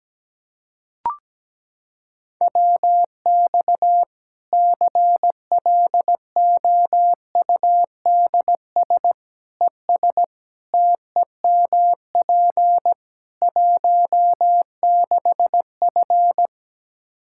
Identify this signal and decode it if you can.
Morse code, heard as a keyed tone, about 17 wpm, 700 Hz. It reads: WX CLOUDS ES TEMP 16F
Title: Practice QSO - Exchange 2 - Info - 17wpm